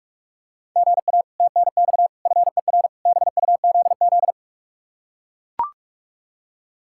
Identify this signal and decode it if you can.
Morse code, heard as a keyed tone, about 44 wpm, 700 Hz. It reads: GA TNX FER BUZZ